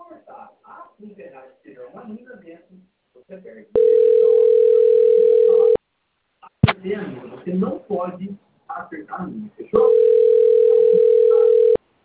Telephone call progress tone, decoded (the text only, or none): ringback tone